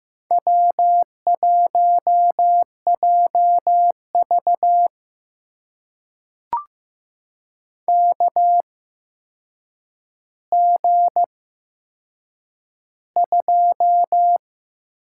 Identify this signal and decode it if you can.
Morse code, heard as a keyed tone, about 15 wpm, spaced as for 7 wpm, 700 Hz. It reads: W1JV KG2
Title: USA Call Signs - 4 Characters - 8x Character Spacing - 15wpm